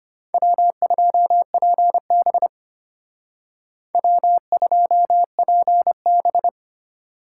Morse code, heard as a keyed tone, about 33 wpm, 700 Hz. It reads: W2PB W2PB